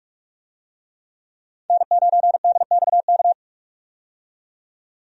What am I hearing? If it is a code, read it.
Morse code, heard as a keyed tone, about 45 wpm, 700 Hz. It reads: N9DXK
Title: USA Call Signs - 5 Characters - 45wpm